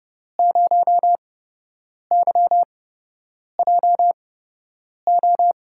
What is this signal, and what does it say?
Morse code, heard as a keyed tone, about 30 wpm, spaced as for 8 wpm, 700 Hz. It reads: 0YJO
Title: USA Call Signs - 5 Characters - 8x Character Spacing - 30wpm